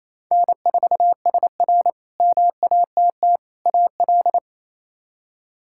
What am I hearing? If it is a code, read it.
Morse code, heard as a keyed tone, about 27 wpm, 700 Hz. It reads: N4SR MATT AL
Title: K1USN SST QSO Practice - 28wpm